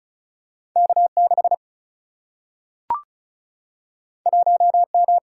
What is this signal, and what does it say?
Morse code, heard as a keyed tone, about 36 wpm, 700 Hz. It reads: K 6 1 M
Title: Single-Letter Single-Number 35wpm